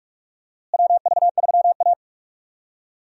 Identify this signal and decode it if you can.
Morse code, heard as a keyed tone, about 46 wpm, 700 Hz. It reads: WV3A
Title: USA Call Signs - 4 Characters - Repeated 3x - 45wpm